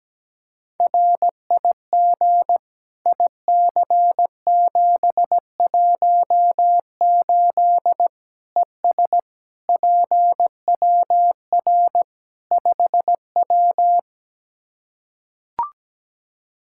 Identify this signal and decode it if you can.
Morse code, heard as a keyed tone, about 17 wpm, 700 Hz. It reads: RIG IC718 ES PWR 5W
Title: K1USN SST QSO Practice 17wpm